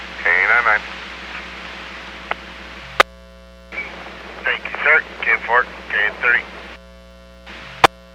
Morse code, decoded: none

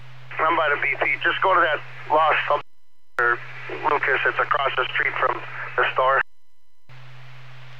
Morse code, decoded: none

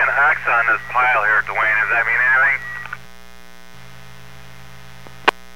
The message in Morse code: none

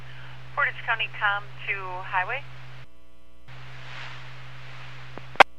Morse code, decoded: none

